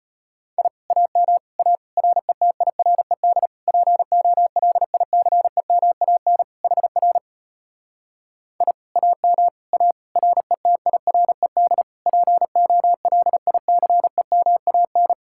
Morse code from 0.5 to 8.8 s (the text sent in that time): I AM A RETIRED POLICEMAN HR I